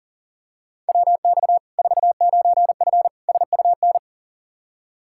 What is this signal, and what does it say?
Morse code, heard as a keyed tone, about 40 wpm, 700 Hz. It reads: WX 49F SUN
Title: Practice QSO - Exchange 2 - Info - 40wpm